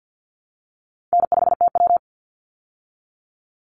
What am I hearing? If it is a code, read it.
Morse code, heard as a keyed tone, about 48 wpm, 700 Hz. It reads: N5TW
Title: Easy Call Signs Repeated 2x - 50wpm